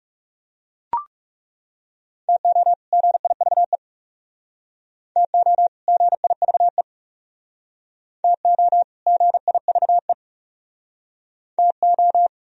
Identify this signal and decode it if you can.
Morse code, heard as a keyed tone, about 40 wpm, 700 Hz. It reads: TO GIVE TO GIVE TO GIVE TO